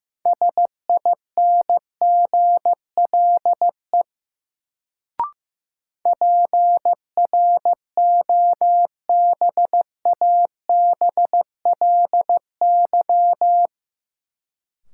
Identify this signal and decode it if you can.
Morse code, heard as a keyed tone, about 16 wpm, 700 Hz. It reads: SINGLE PROBABLY
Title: Sentences from Top 700 Words - New Word Drill - 15wpm